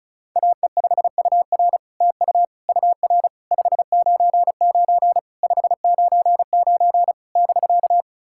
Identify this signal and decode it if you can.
Morse code, heard as a keyed tone, about 35 wpm, 700 Hz. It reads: AE5UR TU UR 599 599 <BK>
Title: SOTA QSO Practice 35wpm